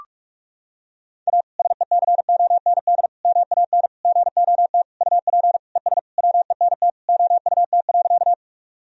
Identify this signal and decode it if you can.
Morse code, heard as a keyed tone, about 47 wpm, 700 Hz. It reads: A SECOND MAN GOT UP ES WENT OUT.
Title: Sentences from Top 300 Words and Abbreviations 45wpm